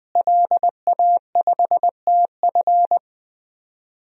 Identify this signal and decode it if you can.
Morse code, heard as a keyed tone, about 20 wpm, 700 Hz. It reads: LA5TF
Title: DXCC-100 Call Signs - 5 Characters - Repeated 2x - 20wpm